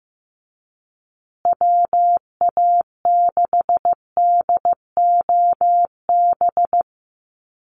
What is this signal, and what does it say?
Morse code, heard as a keyed tone, about 15 wpm, 700 Hz. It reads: WA6DOB
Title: USA Call Signs Repeated 2x 15wpm